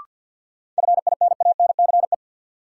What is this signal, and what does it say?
Morse code, heard as a keyed tone, about 53 wpm, 700 Hz. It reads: FINANCE
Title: Sentences from Top 1500 Words - New Word Drill Alt - 50wpm